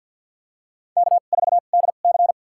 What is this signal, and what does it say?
Morse code, heard as a keyed tone, about 51 wpm, spaced as for 36 wpm, 700 Hz. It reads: K4DC
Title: USA Call Signs - 4 Characters - 2x Character Spacing - 50wpm